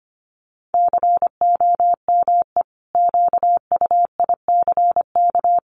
Morse code, heard as a keyed tone, about 25 wpm, 700 Hz. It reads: COME QUICK
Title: Sentences from Top 1300 Words 25wpm